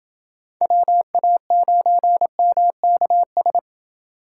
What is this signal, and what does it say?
Morse code, heard as a keyed tone, about 29 wpm, 700 Hz. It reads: WA9MKS